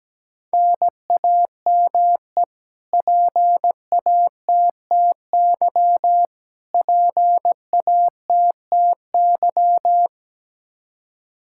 Morse code, heard as a keyed tone, about 17 wpm, 700 Hz. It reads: NAME PATTY PATTY
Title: Practice QSO - Exchange 1 - Beginning - 17wpm